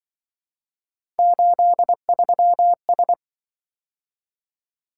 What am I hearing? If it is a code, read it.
Morse code, heard as a keyed tone, about 26 wpm, 700 Hz. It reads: 83S